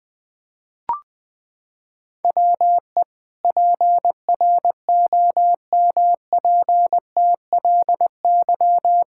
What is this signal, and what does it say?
Morse code, heard as a keyed tone, about 21 wpm, 700 Hz. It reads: WE PROMPTLY
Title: Pangram Challenge v2 - 20wpm